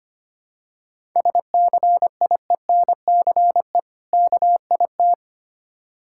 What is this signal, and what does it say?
Morse code, heard as a keyed tone, about 25 wpm, 700 Hz. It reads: SCIENCE KIT
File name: practice / 25wpm / Difficult Words with Hints - 25wpm